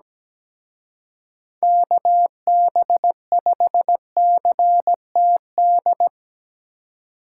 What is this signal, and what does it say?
Morse code, heard as a keyed tone, about 19 wpm, 700 Hz. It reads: KB5CTD
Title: Speed Racing - USA 2x3 Call Signs 17wpm